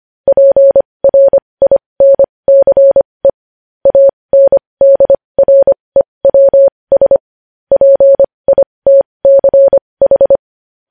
Morse code, heard as a keyed tone, about 25 wpm, spaced as for 20 wpm, 550 Hz. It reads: PRINCE ANDREWS PITCH
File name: podcast / News headlines-1x-25x20